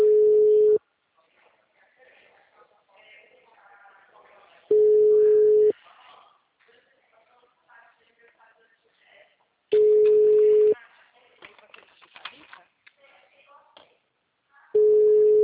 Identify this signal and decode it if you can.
Ringback tone (Europe, 425 Hz), with speech.
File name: out- -2014- - - .98763